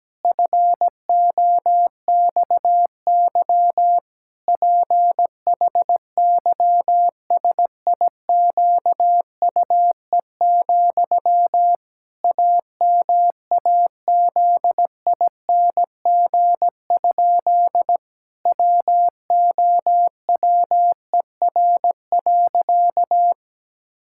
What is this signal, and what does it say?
Morse code, heard as a keyed tone, about 17 wpm, 700 Hz. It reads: FOXY PHYSIQUE, AMAZING? WOWER.
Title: Pangram Challenge v2 - 17wpm